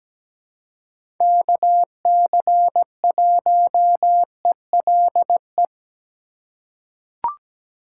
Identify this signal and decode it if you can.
Morse code, heard as a keyed tone, about 17 wpm, 700 Hz. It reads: KC1ELE
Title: USA Call Signs - 6 Characters - 2x Character Spacing - 17wpm